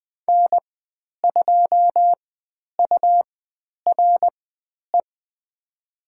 Morse code, heard as a keyed tone, about 20 wpm, spaced as for 10 wpm, 700 Hz. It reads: N2URE
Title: 109 Lesson 18g - Letter m - Call Signs 20wpm